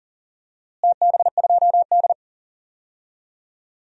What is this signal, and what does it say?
Morse code, heard as a keyed tone, about 40 wpm, 700 Hz. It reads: TB2D
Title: DXCC-100 Call Signs - 4 Characters - Repeated 2x - 40wpm